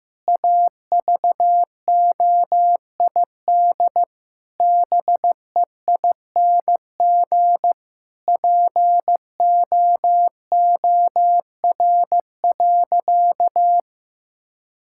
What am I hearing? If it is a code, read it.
Morse code, heard as a keyed tone, about 16 wpm, 700 Hz. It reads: AVOID BEING POOR.
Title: Sentences from Top 1000 Words - Review Part 6 - 15wpm